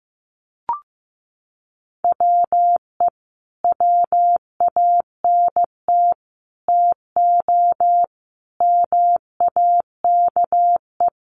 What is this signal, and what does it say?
Morse code, heard as a keyed tone, about 15 wpm, 700 Hz. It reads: WE WANT TO MAKE
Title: Sets of 5 Words Encore2 15wpm